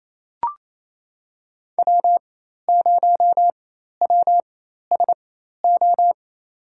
Morse code, heard as a keyed tone, about 28 wpm, spaced as for 13 wpm, 700 Hz. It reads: W0WSO
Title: USA Call Signs - 5 Characters - 4x Character Spacing - 28wpm